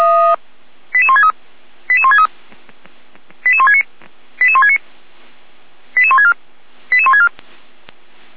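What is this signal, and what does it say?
Five-tone selective call (ZVEI), heard as quick runs of five tones, tones of about 70 ms. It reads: 88163 88163 88169 88169 88164 88164
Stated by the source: ZVEI five-tone call received by SDR, archived by Feuerwehr Orlamünde.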